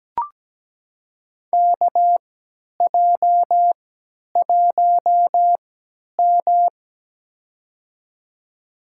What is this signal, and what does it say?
Morse code, heard as a keyed tone, about 17 wpm, spaced as for 10 wpm, 700 Hz. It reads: KJ1M
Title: USA Call Signs - 4 Characters - 3x Character Spacing - 17wpm